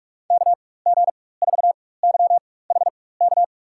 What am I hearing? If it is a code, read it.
Morse code, heard as a keyed tone, about 46 wpm, spaced as for 21 wpm, 700 Hz. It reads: KG4YHK
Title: USA Call Signs - 6 Characters - 4x Character Spacing - 45wpm